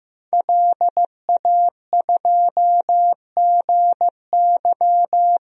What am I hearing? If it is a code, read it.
Morse code, heard as a keyed tone, about 15 wpm, 700 Hz. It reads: LA2GY
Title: DXCC-100 Call Signs - 5 Characters - 15wpm